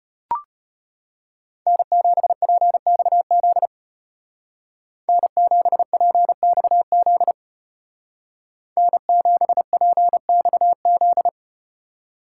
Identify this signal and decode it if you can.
Morse code, heard as a keyed tone, about 34 wpm, 700 Hz. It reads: N7PXZ N7PXZ N7PXZ